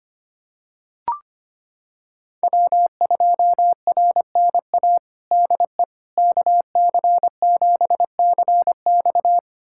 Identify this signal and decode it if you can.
Morse code, heard as a keyed tone, about 25 wpm, 700 Hz. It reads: W2RNA DE KC7CX
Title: USA Callsign DE Callsign - 5 Characters - 25wpm